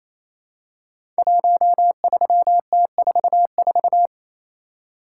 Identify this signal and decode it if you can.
Morse code, heard as a keyed tone, about 29 wpm, 700 Hz. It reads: 13T44
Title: Code Groups - 5 Letter:Numbers - 28wpm